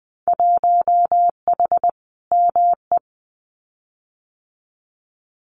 Morse code, heard as a keyed tone, about 20 wpm, 700 Hz. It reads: 1H ME
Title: Winter Field Day Exchanges 20wpm